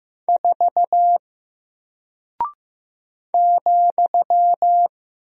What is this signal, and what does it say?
Morse code, heard as a keyed tone, about 15 wpm, 700 Hz. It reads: 4 ,
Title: Single Character (Letter:Number:Punctuation:Prosign) 15wpm